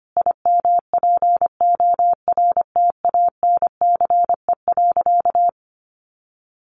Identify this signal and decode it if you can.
Morse code, heard as a keyed tone, about 25 wpm, 700 Hz. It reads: IMPORTANCE.
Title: Sentences from Top 1500 Words 25wpm